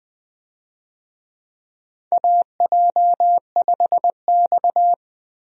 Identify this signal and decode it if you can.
Morse code, heard as a keyed tone, about 21 wpm, 700 Hz. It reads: AJ5X
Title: USA Call Signs - 4 Characters - Repeated 2x - 20wpm